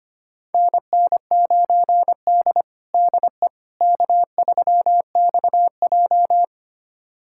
Morse code, heard as a keyed tone, about 25 wpm, 700 Hz. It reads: NN9D DE K3XJ
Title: USA Callsign DE Callsign - 4 Characters - 25wpm